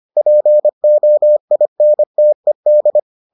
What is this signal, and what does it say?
Morse code, heard as a keyed tone, about 25 wpm, 600 Hz. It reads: POINTED